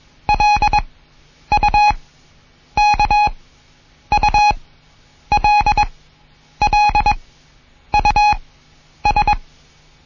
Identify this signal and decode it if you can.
Morse code, heard as a keyed tone, about 21 wpm, spaced as for 9 wpm, 830 Hz. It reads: LUKULLUS
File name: Morsen / Kurs A 20